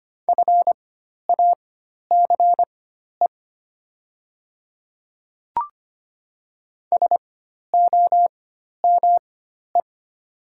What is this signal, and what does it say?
Morse code, heard as a keyed tone, about 25 wpm, spaced as for 12 wpm, 700 Hz. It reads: FACE SOME
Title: 4x Character Spacing - Top 500 Words - 25wpm